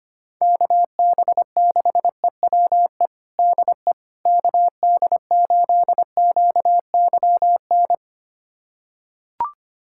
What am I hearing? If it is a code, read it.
Morse code, heard as a keyed tone, about 25 wpm, 700 Hz. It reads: KB6EWE DE KD8QYN
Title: USA Callsign DE Callsign - 6 Characters - 25wpm